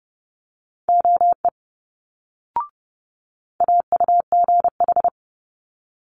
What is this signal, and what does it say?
Morse code, heard as a keyed tone, about 30 wpm, 700 Hz. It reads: OE AUGH
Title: Multi-letter Phonemes Quick Fire 30wpm